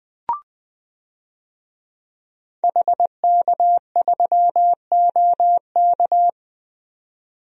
Morse code, heard as a keyed tone, about 20 wpm, 700 Hz. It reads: HK3OK
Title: DXCC-100 Call Signs - 5 Characters - 20wpm